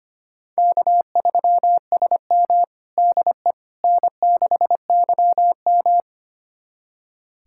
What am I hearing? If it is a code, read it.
Morse code, heard as a keyed tone, about 25 wpm, 700 Hz. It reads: K3SM DE N6YM
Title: USA Callsign DE Callsign - 4 Characters - 25wpm